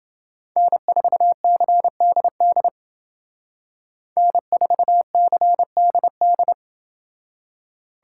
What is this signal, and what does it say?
Morse code, heard as a keyed tone, about 27 wpm, 700 Hz. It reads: N4CDD N4CDD